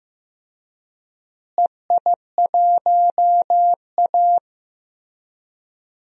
Morse code, heard as a keyed tone, about 16 wpm, 700 Hz. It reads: EI1A
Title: DXCC-100 Call Signs - 4 Characters - Repeated 3x - 15wpm